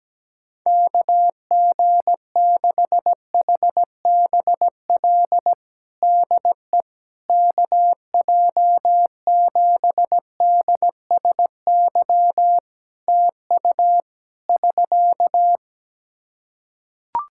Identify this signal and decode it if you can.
Morse code, heard as a keyed tone, about 17 wpm, 700 Hz. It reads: KG6HBL DE KJ7DSY TU <SK>
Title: K1USN SST QSO Practice 17wpm